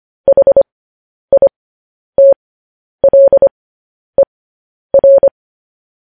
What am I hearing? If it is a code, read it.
Morse code, heard as a keyed tone, about 25 wpm, spaced as for 10 wpm, 550 Hz. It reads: HITLER
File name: podcast / News headlines-1x-25x10